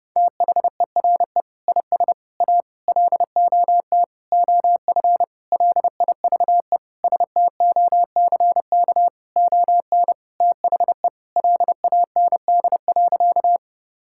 Morse code, heard as a keyed tone, about 30 wpm, 700 Hz. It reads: THERE IS A LOT OF LIVE STOCK ON THE LAND.
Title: Sentences from Top 1000 Words - Review Part 5 - 30wpm